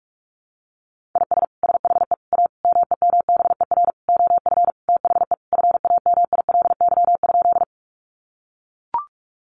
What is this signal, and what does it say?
Morse code, heard as a keyed tone, about 44 wpm, 700 Hz. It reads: IS SHE A MEMBER OF THE FAMILY?
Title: Sentences from Top 1100 Words 45wpm